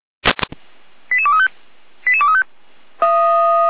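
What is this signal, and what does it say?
Five-tone selective call (ZVEI), heard as quick runs of five tones, tones of about 70 ms. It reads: 88235 88235
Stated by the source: ZVEI five-tone selective call received on an SDR.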